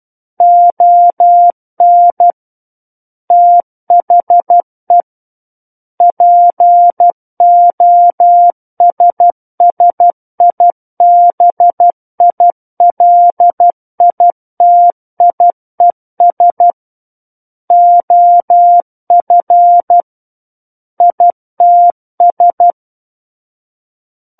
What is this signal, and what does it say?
Morse code, heard as a keyed tone, about 12 wpm, 700 Hz. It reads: ON THE POSSIBILITIES OF ITS
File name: Book One / 03-Chapter 3 - 12 WPM